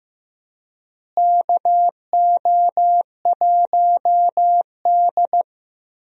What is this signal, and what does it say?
Morse code, heard as a keyed tone, about 15 wpm, 700 Hz. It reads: KO1D
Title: USA Call Signs - 4 Characters - Repeated 2x - 15wpm